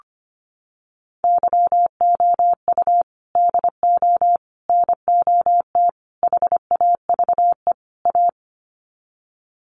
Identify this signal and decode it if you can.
Morse code, heard as a keyed tone, about 25 wpm, 700 Hz. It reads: YOU DO NOT HAVE A
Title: Sets of 5 Words 25wpm